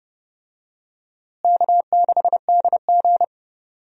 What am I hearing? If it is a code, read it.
Morse code, heard as a keyed tone, about 30 wpm, 700 Hz. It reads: K6DG
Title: USA Call Signs - 4 Characters - Repeated 3x - 30wpm